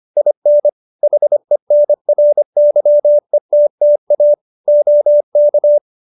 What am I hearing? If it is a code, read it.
Morse code, heard as a keyed tone, about 25 wpm, 590 Hz. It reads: IN HENRYETTA OK